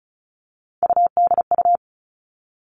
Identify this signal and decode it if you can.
Morse code, heard as a keyed tone, about 36 wpm, 700 Hz. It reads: UDU
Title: Three Challenging Letter-Number Combinations 35wpm